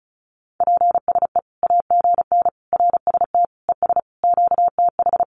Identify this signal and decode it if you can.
Morse code, heard as a keyed tone, about 34 wpm, 700 Hz. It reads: PSE AGN RST ES QTH
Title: Common QSO Phrases 35wpm